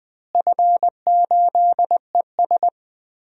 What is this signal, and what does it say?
Morse code, heard as a keyed tone, about 21 wpm, 700 Hz. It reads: F8ES
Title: DXCC-100 Call Signs - 4 Characters - Repeated 2x - 20wpm